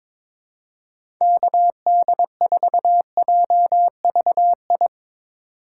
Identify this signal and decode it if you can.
Morse code, heard as a keyed tone, about 23 wpm, 700 Hz. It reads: KD4JVI
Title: USA Call Signs - 3x Character Spacing - 22wpm